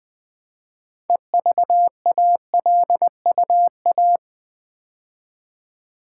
Morse code, heard as a keyed tone, about 20 wpm, 700 Hz. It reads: EVALUA
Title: Build A Word - 2 Letters At A Time - 8-Letter Words - 20wpm